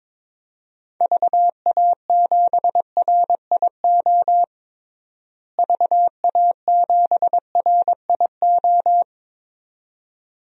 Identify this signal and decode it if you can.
Morse code, heard as a keyed tone, about 22 wpm, 700 Hz. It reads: VA7RIO VA7RIO
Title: DXCC-100 Call Signs - 6 Characters - Repeated 2x - 22wpm